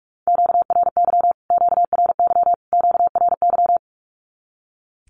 Morse code, heard as a keyed tone, about 44 wpm, 700 Hz. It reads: QRY QRY QRY
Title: Q-Codes Repeated 3x 45wpm